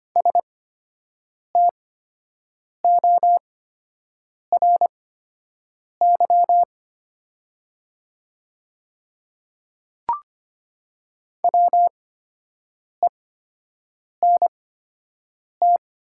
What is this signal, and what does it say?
Morse code, heard as a keyed tone, about 25 wpm, spaced as for 7 wpm, 700 Hz. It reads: STORY WENT
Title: 8x Character Spacing - Top 300 Words - 25wpm